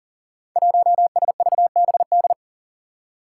Morse code, heard as a keyed tone, about 42 wpm, 700 Hz. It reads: 1SVBD
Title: Five Challenging Letter-Number Combinations 40wpm